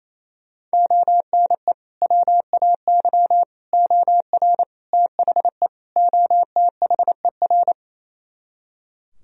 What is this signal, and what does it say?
Morse code, heard as a keyed tone, about 38 wpm, spaced as for 33 wpm, 700 Hz. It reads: ONE WAY OR THE OTHER
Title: Speed-Racing - Sets of 5 Words 28wpm